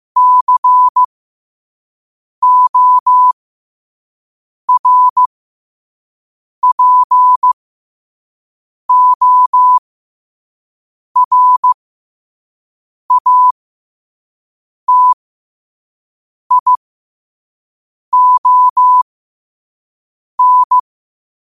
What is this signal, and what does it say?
Morse code, heard as a keyed tone, about 15 wpm, spaced as for 5 wpm, 1000 Hz. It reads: CORPORATION
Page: qotd-at-5-wpm